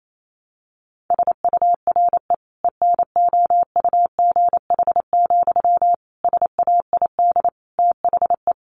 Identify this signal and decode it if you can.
Morse code, heard as a keyed tone, about 28 wpm, 700 Hz. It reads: SURE ENOUGH, SAID THE
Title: Sentences from Top 600 Words 28wpm